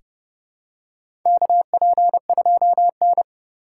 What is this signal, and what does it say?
Morse code, heard as a keyed tone, about 32 wpm, 700 Hz. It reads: KP2N
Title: K1USN SST QSO Practice - 30wpm